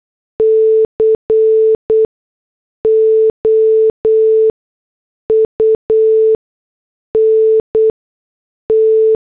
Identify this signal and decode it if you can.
Morse code, heard as a keyed tone, about 8 wpm, spaced as for 6 wpm, 440 Hz. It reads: COUNT
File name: morse / headlines